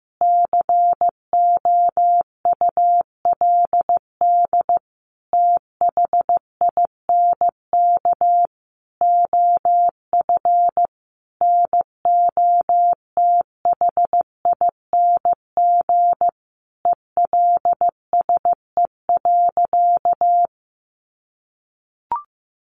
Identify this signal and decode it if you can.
Morse code, heard as a keyed tone, about 15 wpm, 700 Hz. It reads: COULD THINK OF NOTHING ELSE.
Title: Sentences from Top 800 Words 15wpm